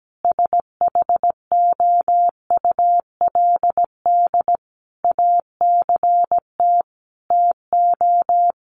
Morse code, heard as a keyed tone, about 17 wpm, 700 Hz. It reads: SHOULD ACT TO
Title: Sentences from Top 1100 Words 17wpm